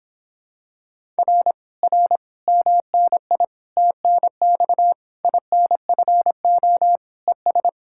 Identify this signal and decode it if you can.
Morse code, heard as a keyed tone, about 26 wpm, 700 Hz. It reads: R R MNI TNX INFO ES